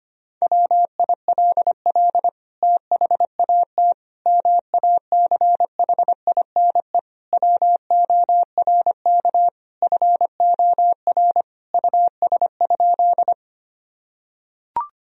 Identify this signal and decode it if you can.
Morse code, heard as a keyed tone, about 25 wpm, 700 Hz. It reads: WILL THAT MACHINE WORK FOR US?
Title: Sentences from Top 1000 Words - Review Part 2 - 25wpm